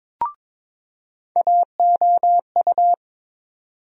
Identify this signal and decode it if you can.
Morse code, heard as a keyed tone, about 22 wpm, 700 Hz. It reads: AOU
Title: USA Call Sign Postfixes - 3 Letters 22wpm